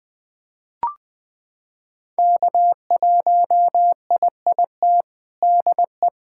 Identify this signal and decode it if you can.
Morse code, heard as a keyed tone, about 21 wpm, 700 Hz. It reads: K1IIT DE
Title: USA Callsign DE Callsign - 5 Characters - 20wpm